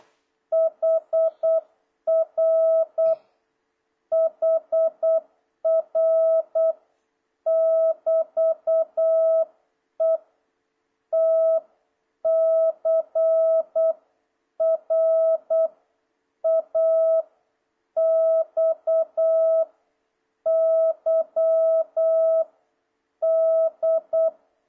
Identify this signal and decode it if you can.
Morse code, heard as a keyed tone, about 8 wpm, 660 Hz. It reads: HR HR <BT> E T C R A X Y D
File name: jh8cbh / 2026cw